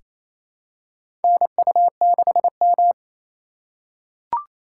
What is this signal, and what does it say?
Morse code, heard as a keyed tone, about 28 wpm, 700 Hz. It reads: NU6M
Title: USA Call Signs - 2x Character Spacing - 28wpm